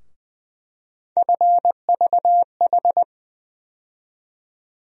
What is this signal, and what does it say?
Morse code, heard as a keyed tone, about 20 wpm, 700 Hz. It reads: FVH
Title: USA Call Sign Postfixes - 3 Letters 20wpm